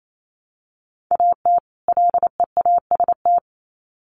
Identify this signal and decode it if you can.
Morse code, heard as a keyed tone, about 28 wpm, 700 Hz. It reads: AT LEAST
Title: Sets of 2 Words 28wpm